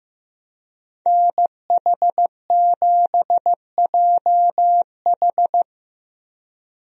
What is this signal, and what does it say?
Morse code, heard as a keyed tone, about 16 wpm, 700 Hz. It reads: NH7JH
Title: USA Call Signs - 5 Characters - Repeated 2x - 15wpm